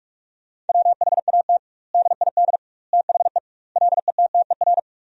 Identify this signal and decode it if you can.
Morse code, heard as a keyed tone, about 43 wpm, 700 Hz. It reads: WHAT DID THE LETTER
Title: 2x Word-Spacing - Sentences from Top 500 Words - 45wpm